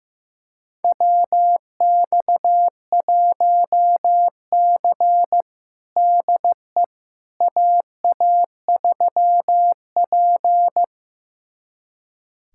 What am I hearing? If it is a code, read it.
Morse code, heard as a keyed tone, about 16 wpm, 700 Hz. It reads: WX1C DE AA3P
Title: USA Callsign DE Callsign - 4 Characters - 15wpm